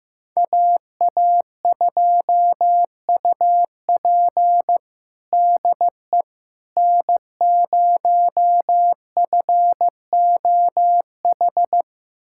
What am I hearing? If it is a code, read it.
Morse code, heard as a keyed tone, about 15 wpm, 700 Hz. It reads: AA2UP DE N0FOH
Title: USA Callsign DE Callsign - 5 Characters - 15wpm